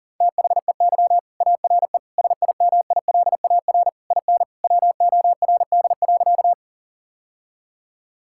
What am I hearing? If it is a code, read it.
Morse code, heard as a keyed tone, about 37 wpm, 700 Hz. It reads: THEY ARE SIMILAR IN WORD.
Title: Sentences from Top 1000 Words - Review Part 5 - 40wpm